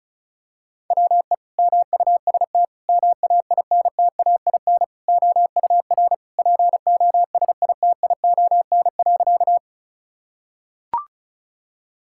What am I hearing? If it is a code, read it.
Morse code, heard as a keyed tone, about 34 wpm, 700 Hz. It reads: WE MUST MAINTAIN OUR POSITION.
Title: Sentences from Top 1000 Words - Review Part 5 - 35wpm